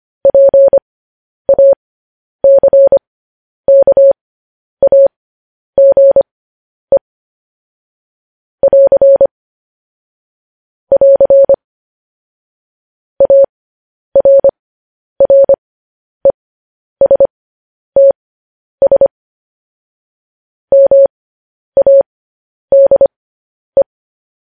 Morse code, heard as a keyed tone, about 25 wpm, spaced as for 10 wpm, 550 Hz. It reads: PACKAGE <AR> <AR> ARRESTS MADE